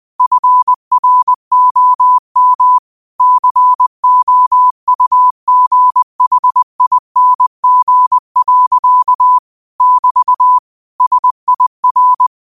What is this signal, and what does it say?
Morse code, heard as a keyed tone, about 20 wpm, 1000 Hz. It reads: FROM COUGHING. <BT> SIR
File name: qotdfiles / 20wpm